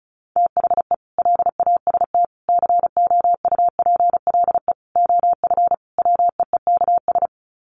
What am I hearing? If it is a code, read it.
Morse code, heard as a keyed tone, about 34 wpm, 700 Hz. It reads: THE LAST COUPLE OF WEEKS
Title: Sets of 5 Words Encore2 35wpm